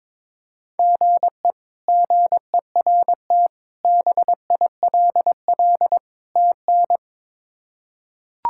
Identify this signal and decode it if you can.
Morse code, heard as a keyed tone, about 22 wpm, 700 Hz. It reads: GE GERT BILL TN
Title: K1USN SST QSO Practice - 22wpm